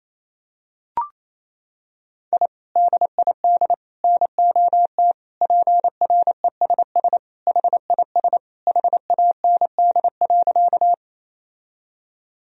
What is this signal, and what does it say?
Morse code, heard as a keyed tone, about 28 wpm, 700 Hz. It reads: I DID NOT PRESS HIS HAND.
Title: Sentences from Top 1000 Words - Review Part 1 - 28wpm